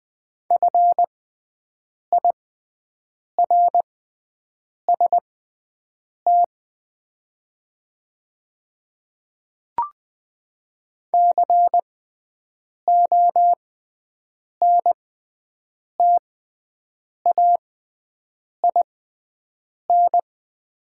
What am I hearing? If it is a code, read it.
Morse code, heard as a keyed tone, about 20 wpm, spaced as for 7 wpm, 700 Hz. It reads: FIRST CONTAIN
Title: 6x Character Spacing - Top 500 Words - 20wpm